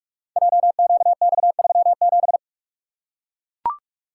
Morse code, heard as a keyed tone, about 44 wpm, 690 Hz. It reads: JQX37
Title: Code Groups - 5 Letter:Numbers - 45wpm